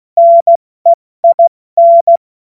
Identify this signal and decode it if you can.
Morse code, heard as a keyed tone, about 16 wpm, 680 Hz. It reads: NEIN
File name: Morsekurs / Witze / Witz 25. Mai 2021, 70 F80